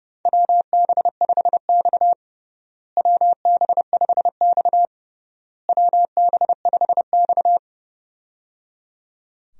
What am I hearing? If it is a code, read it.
Morse code, heard as a keyed tone, about 29 wpm, 700 Hz. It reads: WB5X WB5X WB5X
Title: USA Call Signs - 4 Characters - Repeated 3x - 30wpm